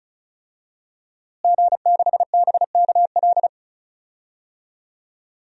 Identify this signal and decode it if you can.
Morse code, heard as a keyed tone, about 34 wpm, 700 Hz. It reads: G6BKL
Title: DXCC-100 Call Signs - 5 Characters - Repeated 3x - 35wpm